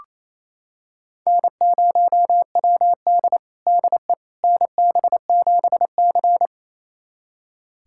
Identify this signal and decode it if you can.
Morse code, heard as a keyed tone, about 28 wpm, 700 Hz. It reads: N0WD DE NB7C
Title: USA Callsign DE Callsign - 4 Characters - 28wpm